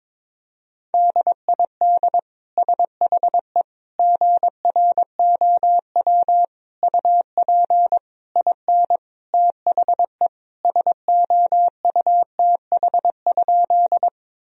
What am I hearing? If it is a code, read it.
Morse code, heard as a keyed tone, about 22 wpm, 700 Hz. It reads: DID SHE GROW UP IN THE SOUTH?
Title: Sentences from Top 1000 Words - Review Part 2 - 22wpm